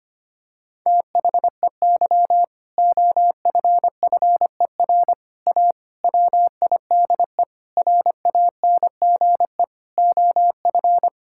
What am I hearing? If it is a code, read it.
Morse code, heard as a keyed tone, about 25 wpm, 700 Hz. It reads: THEY OFFER A WIDE RANGE OF
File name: practice / 25wpm / Sentences with 4-Word Phrases - 25wpm